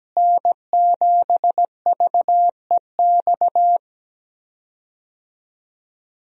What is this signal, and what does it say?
Morse code, heard as a keyed tone, about 17 wpm, 700 Hz. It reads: N7VEX
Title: USA Call Signs - 5 Characters - Repeated 2x - 17wpm